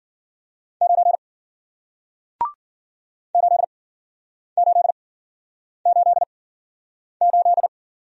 Morse code, heard as a keyed tone, about 53 wpm, spaced as for 9 wpm, 700 Hz. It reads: 9 8888